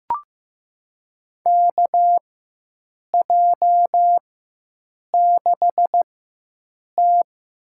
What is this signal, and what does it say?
Morse code, heard as a keyed tone, about 15 wpm, spaced as for 7 wpm, 700 Hz. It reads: KJ6T
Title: USA Call Signs - 6 Characters - 4x Character Spacing - 15wpm